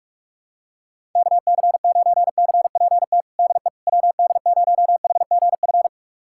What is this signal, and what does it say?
Morse code, heard as a keyed tone, about 42 wpm, 700 Hz. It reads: KC9CPT DE WD0HGF
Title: USA Callsign DE Callsign - 6 Characters - 45wpm